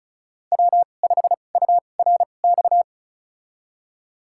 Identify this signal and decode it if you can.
Morse code, heard as a keyed tone, about 36 wpm, spaced as for 25 wpm, 700 Hz. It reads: W5URX
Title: USA Call Signs - 5 Characters - 2x Character Spacing - 35wpm